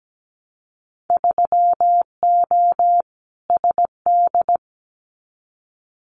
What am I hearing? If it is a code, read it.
Morse code, heard as a keyed tone, about 17 wpm, 700 Hz. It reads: 3O SD
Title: Winter Field Day Exchanges 17wpm